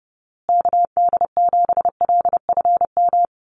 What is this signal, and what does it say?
Morse code, heard as a keyed tone, about 30 wpm, 700 Hz. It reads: KD7LFM
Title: USA Call Signs Repeated 2x 30wpm